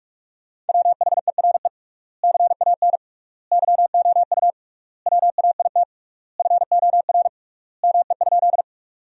Morse code, heard as a keyed tone, about 42 wpm, 700 Hz. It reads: WHERE CAN YOU WAIT FOR ME?
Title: 3x Word-Spacing - Sentences from Top 500 Words - 45wpm